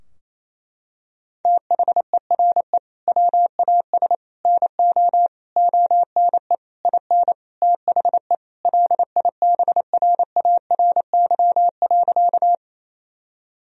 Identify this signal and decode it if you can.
Morse code, heard as a keyed tone, about 28 wpm, 700 Hz. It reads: THERE WAS NO ONE IN THE LIBRARY.
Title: Sentences from Top 1000 Words - Review Part 4 - 28wpm